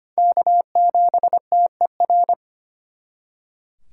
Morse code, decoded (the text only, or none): K7TER